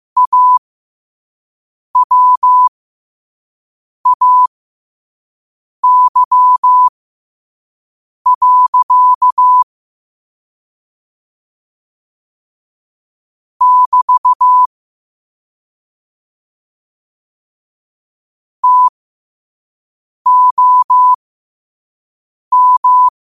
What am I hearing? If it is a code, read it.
Morse code, heard as a keyed tone, about 15 wpm, spaced as for 5 wpm, 1000 Hz. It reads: AWAY. <BT> TOM